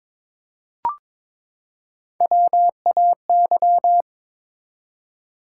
Morse code, heard as a keyed tone, about 22 wpm, 700 Hz. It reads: WAY
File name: Build A Phrase - 2-Word Phrases - 22wpm